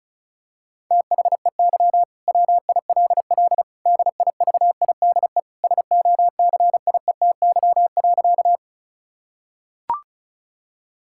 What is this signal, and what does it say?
Morse code, heard as a keyed tone, about 36 wpm, 690 Hz. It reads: THEY WILL DIVIDE SOCIETY.
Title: Sentences from Top 1600 Words 35wpm